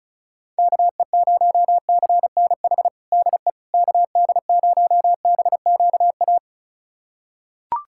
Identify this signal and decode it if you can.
Morse code, heard as a keyed tone, about 36 wpm, 700 Hz. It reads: KE0CNH DE KD0BQA
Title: USA Callsign DE Callsign - 6 Characters - 35wpm